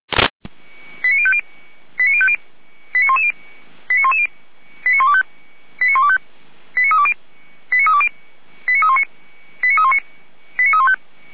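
Five-tone selective call (ZVEI), heard as quick runs of five tones, tones of about 70 ms. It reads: 89050 89110 89110 89125 89125 89320 89320 89319 89319 89315
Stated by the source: ZVEI five-tone selective call received with an SDR.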